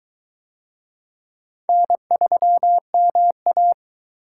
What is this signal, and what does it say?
Morse code, heard as a keyed tone, about 23 wpm, 700 Hz. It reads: N3MA